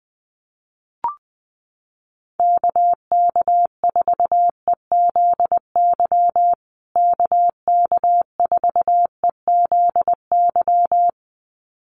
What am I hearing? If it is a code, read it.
Morse code, heard as a keyed tone, about 19 wpm, 700 Hz. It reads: KK4EZY KK4EZY
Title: USA Call Signs Repeated 2x 20wpm